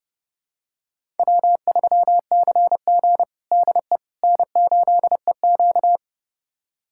Morse code, heard as a keyed tone, about 32 wpm, 700 Hz. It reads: W3CG DE N8EQ
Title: USA Callsign DE Callsign - 4 Characters - 30wpm